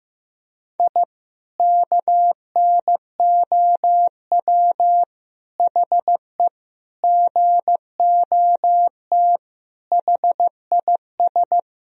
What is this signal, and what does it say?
Morse code, heard as a keyed tone, about 15 wpm, 700 Hz. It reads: I KNOW HE GOT HIS
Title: Sentences from Top 1000 Words - Review Part 3 - 15wpm